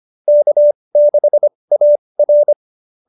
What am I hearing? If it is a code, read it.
Morse code, heard as a keyed tone, about 25 wpm, spaced as for 20 wpm, 570 Hz. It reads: K6AR
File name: QSO307 20